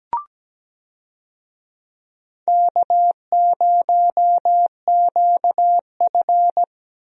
Morse code, heard as a keyed tone, about 17 wpm, 700 Hz. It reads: K0QF
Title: USA Call Signs - 4 Characters - Repeated 2x - 17wpm